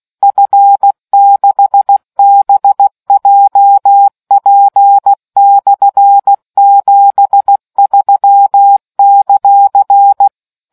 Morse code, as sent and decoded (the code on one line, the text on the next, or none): ..-. -.... -... .--- .--. -..-. --... ...-- -.-.-.
F6BJP/73;